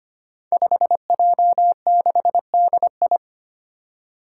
Morse code, heard as a keyed tone, about 25 wpm, 700 Hz. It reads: 5J6DI
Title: Five Challenging Letter-Number Combinations 25wpm